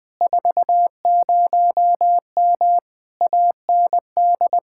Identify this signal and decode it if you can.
Morse code, heard as a keyed tone, about 20 wpm, 700 Hz. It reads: 40M AND
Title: Speed-Racing Common QSO Phrases - 20wpm